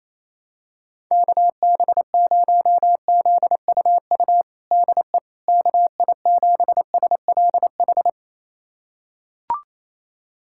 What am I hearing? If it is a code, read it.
Morse code, heard as a keyed tone, about 28 wpm, 700 Hz. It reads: KB0ZUU DE KI7SLH
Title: USA Callsign DE Callsign - 6 Characters - 28wpm